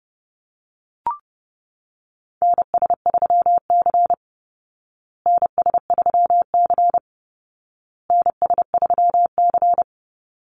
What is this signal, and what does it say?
Morse code, heard as a keyed tone, about 30 wpm, 700 Hz. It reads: NS3C NS3C NS3C
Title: Call Signs Repeated 2x - 30wpm